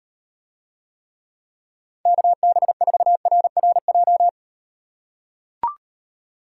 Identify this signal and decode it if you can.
Morse code, heard as a keyed tone, about 38 wpm, 700 Hz. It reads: KB4RRJ